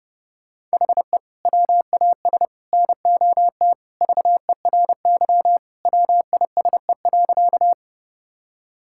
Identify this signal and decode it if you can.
Morse code, heard as a keyed tone, about 30 wpm, 700 Hz. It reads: HE WAS NOT VERY WISE.
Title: Sentences from Top 1600 Words 30wpm